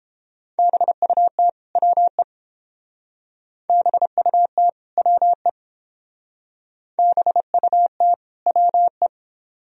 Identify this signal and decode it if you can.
Morse code, heard as a keyed tone, about 30 wpm, 700 Hz. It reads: BUT WE BUT WE BUT WE